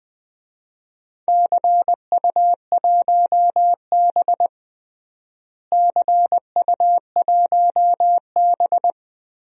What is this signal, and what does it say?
Morse code, heard as a keyed tone, about 20 wpm, 700 Hz. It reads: CU1B CU1B
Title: DXCC-100 Call Signs - 4 Characters - Repeated 3x - 20wpm